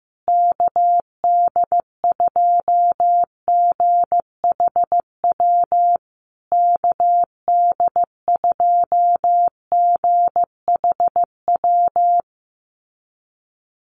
Morse code, heard as a keyed tone, about 15 wpm, 700 Hz. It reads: KD2GHW KD2GHW
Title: USA Call Signs Repeated 3x 15wpm